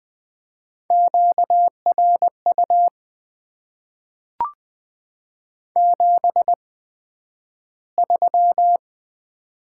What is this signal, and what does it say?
Morse code, heard as a keyed tone, about 20 wpm, spaced as for 11 wpm, 700 Hz. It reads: QRU 73
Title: Top 90 QSO Elements - 8x Character Spacing - 20wpm